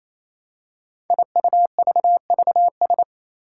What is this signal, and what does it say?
Morse code, heard as a keyed tone, about 27 wpm, 700 Hz. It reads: IUVVS
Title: Five Challenging Letter-Number Combinations 28wpm